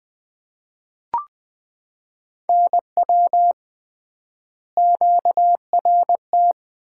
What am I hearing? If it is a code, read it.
Morse code, heard as a keyed tone, about 20 wpm, 700 Hz. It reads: NW QRT
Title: Common QSO Phrases - 3x Word-Spacing - 20wpm